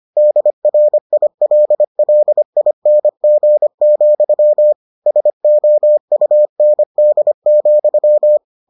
Morse code, heard as a keyed tone, about 25 wpm, 600 Hz. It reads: DRILLING, SOUND,